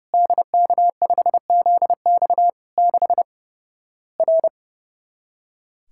Morse code, heard as a keyed tone, about 30 wpm, 700 Hz. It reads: DK5ZX 6 R
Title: Worked All Europe DX Contest - QTC Exchange - 30wpm